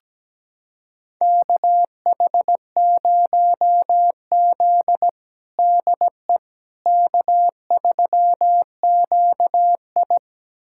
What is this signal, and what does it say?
Morse code, heard as a keyed tone, about 17 wpm, 700 Hz. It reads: KH0Z DE K3QI
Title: USA Callsign DE Callsign - 4 Characters - 17wpm